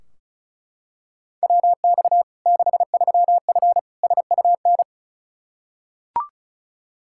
Morse code, heard as a keyed tone, about 36 wpm, 700 Hz. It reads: WX 63F SUN
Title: Practice QSO - Exchange 2 - Info - 35wpm